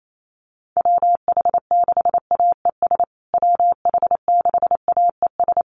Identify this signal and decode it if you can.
Morse code, heard as a keyed tone, about 28 wpm, 700 Hz. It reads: WH6AES WH6AES
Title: USA Call Signs Repeated 3x 28wpm